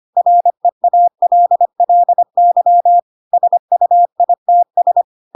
Morse code, heard as a keyed tone, about 25 wpm, 700 Hz. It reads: REALLY SUITS
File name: Pinocchio / Pinocchio ch4 25 25 ik1hge